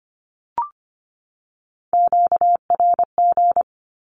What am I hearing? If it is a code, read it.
Morse code, heard as a keyed tone, about 25 wpm, 700 Hz. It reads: QRG
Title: Q-Codes 25wpm